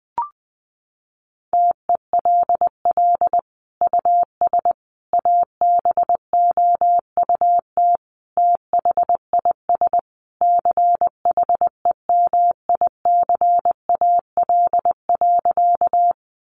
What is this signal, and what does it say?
Morse code, heard as a keyed tone, about 20 wpm, 700 Hz. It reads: TELL US ABOUT THIS CHEMICAL.
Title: Sentences from Top 1200 Words 20wpm